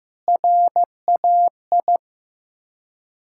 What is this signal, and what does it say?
Morse code, heard as a keyed tone, about 15 wpm, 700 Hz. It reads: RAI
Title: Ontario QSO Party Exchange 15wpm